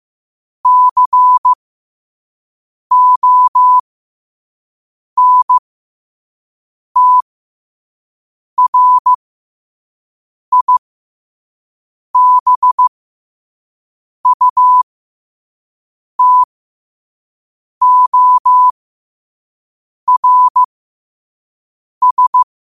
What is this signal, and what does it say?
Morse code, heard as a keyed tone, about 15 wpm, spaced as for 5 wpm, 1000 Hz. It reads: CONTRIBUTORS
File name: qotdfiles / 5wpm